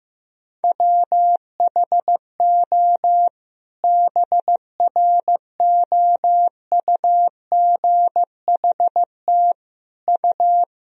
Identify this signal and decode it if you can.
Morse code, heard as a keyed tone, about 15 wpm, 700 Hz. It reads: WHO BROUGHT U
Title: Sentences from Top 500 Words and Abbreviations 15wpm